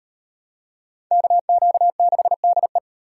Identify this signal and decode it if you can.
Morse code, heard as a keyed tone, about 38 wpm, 700 Hz. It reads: KQ6DE